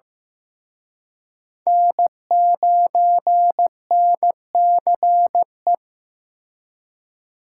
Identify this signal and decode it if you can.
Morse code, heard as a keyed tone, about 18 wpm, 700 Hz. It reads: N9NCE
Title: Speed Racing - USA 1x3 Call Signs 15wpm